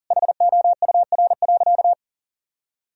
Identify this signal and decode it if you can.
Morse code, heard as a keyed tone, about 37 wpm, 700 Hz. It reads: HOUR.
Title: Sentences from Top 1000 Words - Review Part 4 - 40wpm